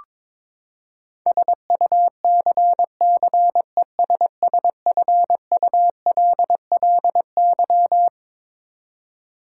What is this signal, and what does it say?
Morse code, heard as a keyed tone, about 21 wpm, 700 Hz. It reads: SUCCESSFULLY
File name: Twelve Letter Words - 22wpm